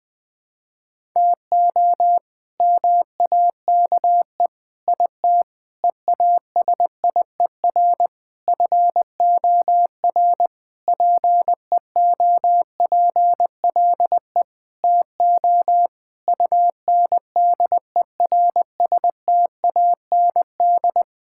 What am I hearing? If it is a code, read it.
Morse code, heard as a keyed tone, about 20 wpm, 700 Hz. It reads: TO MAKE IT EASIER FOR PEOPLE TO UNDERSTAND
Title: Sentences with 7-Word Phrases - 20wpm